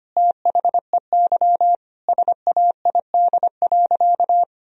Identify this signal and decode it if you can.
Morse code, heard as a keyed tone, about 25 wpm, 700 Hz. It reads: THEY SAID.
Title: Sentences from Top 1000 Words - Review Part 3 - 25wpm